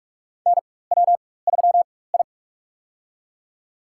Morse code, heard as a keyed tone, about 46 wpm, spaced as for 21 wpm, 700 Hz. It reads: NW3I
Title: USA Call Signs - 4x Character Spacing - 45wpm